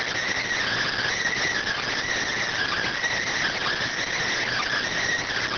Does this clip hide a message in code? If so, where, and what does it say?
Yes: Emergency Alert System (SAME) tones (header bursts) from 0.6 to 4.8 s, heard as harsh buzzing bursts.